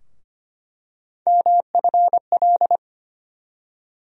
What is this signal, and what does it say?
Morse code, heard as a keyed tone, about 25 wpm, 700 Hz. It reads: MFL
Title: USA Call Sign Postfixes - 3 Letters 25wpm